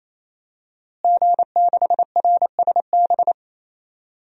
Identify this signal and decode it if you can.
Morse code, heard as a keyed tone, about 28 wpm, 700 Hz. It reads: G6RSB
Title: DXCC-100 Call Signs - 5 Characters - Repeated 3x - 28wpm